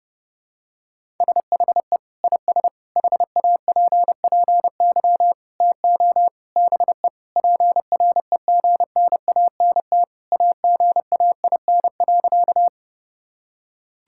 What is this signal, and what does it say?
Morse code, heard as a keyed tone, about 32 wpm, 700 Hz. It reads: SHE IS HAPPY TO BE PREGNANT AGAIN.
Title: Sentences from Top 1000 Words - Review Part 6 - 30wpm